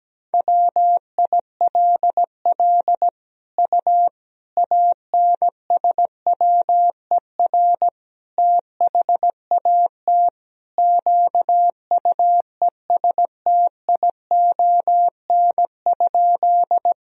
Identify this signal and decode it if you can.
Morse code, heard as a keyed tone, about 17 wpm, 700 Hz. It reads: WILL U ANSWER THAT QUESTION?
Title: Sentences from Top 400 Words and Abbreviations 17wpm